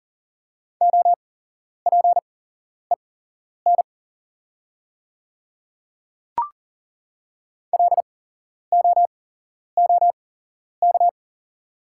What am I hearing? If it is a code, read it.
Morse code, heard as a keyed tone, about 39 wpm, spaced as for 11 wpm, 690 Hz. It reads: OPEN LOOK
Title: 8x Character Spacing - Top 300 Words - 40wpm